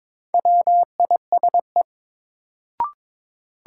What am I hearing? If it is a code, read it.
Morse code, heard as a keyed tone, about 22 wpm, 700 Hz. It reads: WISE
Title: Sentences from Top 1600 Words - New Word Drill - 22wpm